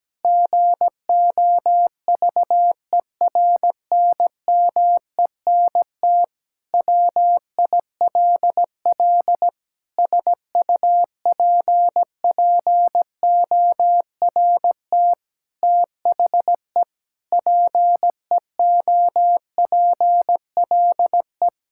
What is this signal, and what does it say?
Morse code, heard as a keyed tone, about 17 wpm, 700 Hz. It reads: GOVERNMENT WILL SUPPORT THE PEOPLE
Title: Sentences from Top 1000 Words - Review Part 5 - 17wpm